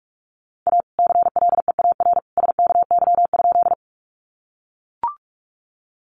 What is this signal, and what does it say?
Morse code, heard as a keyed tone, about 44 wpm, 700 Hz. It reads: A CLEAR SKY?
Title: Sentences from Top 1200 Words 45wpm